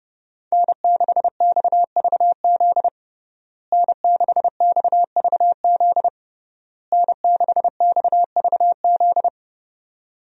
Morse code, heard as a keyed tone, about 30 wpm, 700 Hz. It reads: N6XVZ N6XVZ N6XVZ
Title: USA Call Signs - 5 Characters - Repeated 3x - 30wpm